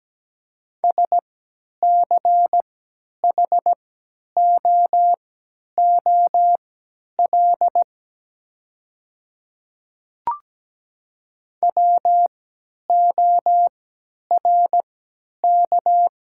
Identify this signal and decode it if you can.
Morse code, heard as a keyed tone, about 17 wpm, spaced as for 10 wpm, 700 Hz. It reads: SCHOOL WORK
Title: 3x Character Spacing - Top 300 Words - 17wpm